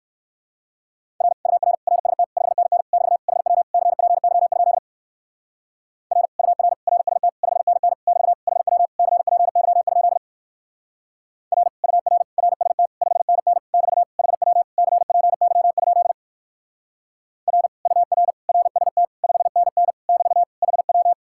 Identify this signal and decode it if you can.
Morse code, heard as a keyed tone, about 62 wpm, 700 Hz. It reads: R UR RST 5NN <BT> HW CPY? R UR RST 5NN <BT> HW CPY? R UR RST 5NN <BT> HW CPY? R UR RST 5NN <BT> HW